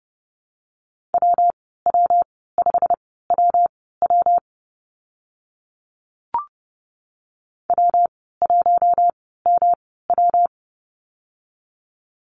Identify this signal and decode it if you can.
Morse code, heard as a keyed tone, about 30 wpm, spaced as for 17 wpm, 700 Hz. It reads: WW5WW W1MW
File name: Lesson 19g - Letter w - Call Signs 30wpm